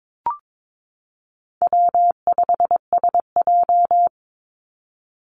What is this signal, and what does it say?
Morse code, heard as a keyed tone, about 22 wpm, 700 Hz. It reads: W5SJ
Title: Call Signs Repeated 3x - 22wpm